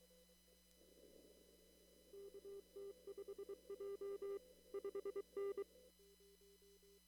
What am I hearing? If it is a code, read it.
Morse code, heard as a keyed tone, about 23 wpm, 400 Hz. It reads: KT5J 5N
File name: CW / kt5j wpx cw-op-k5pi